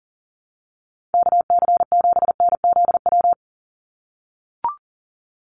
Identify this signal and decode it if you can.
Morse code, heard as a keyed tone, about 38 wpm, 700 Hz. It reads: KC7NZW
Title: USA Call Signs 40wpm